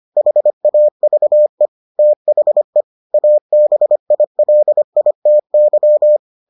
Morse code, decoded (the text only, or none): HAVE THE ABILITY